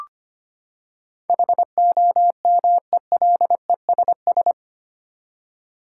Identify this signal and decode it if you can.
Morse code, heard as a keyed tone, about 25 wpm, 700 Hz. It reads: HOMELESS
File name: Eight Letter Words - 25wpm